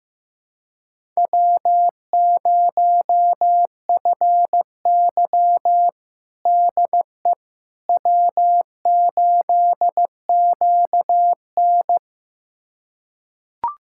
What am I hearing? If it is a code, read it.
Morse code, heard as a keyed tone, about 16 wpm, 700 Hz. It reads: W0FY DE W8QN
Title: USA Callsign DE Callsign - 4 Characters - 15wpm